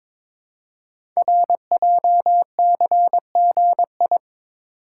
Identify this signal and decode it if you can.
Morse code, heard as a keyed tone, about 23 wpm, 700 Hz. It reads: RJCGI
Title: Code Groups - 5 Letters - 22wpm